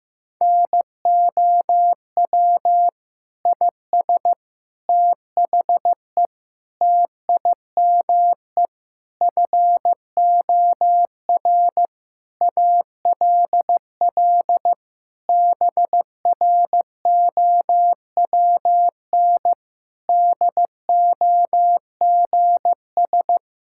Morse code, heard as a keyed tone, about 15 wpm, 700 Hz. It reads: NOW IS THE TIME FOR ALL BROWN DOGS
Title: Pangram Challenge v2 - 15wpm